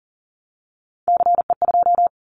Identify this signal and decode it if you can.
Morse code, heard as a keyed tone, about 42 wpm, 700 Hz. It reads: CE2
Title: Callsign Prefixes for Top 100 DXCC - 40wpm